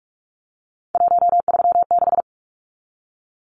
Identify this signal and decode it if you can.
Morse code, heard as a keyed tone, about 44 wpm, 700 Hz. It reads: 136
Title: Numbers - 3-Digit 45wpm